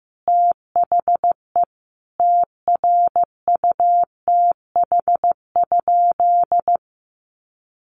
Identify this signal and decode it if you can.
Morse code, heard as a keyed tone, about 15 wpm, 700 Hz. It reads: THE TRUTH?
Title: Sentences from Top 1400 Words 15wpm